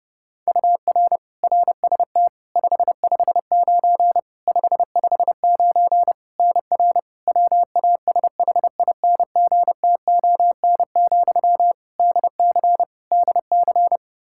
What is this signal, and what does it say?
Morse code, heard as a keyed tone, about 30 wpm, 700 Hz. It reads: UR RST 559 559 NR WASHINGTON, DC DC
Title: Practice QSO - Exchange 1 - Beginning - 30wpm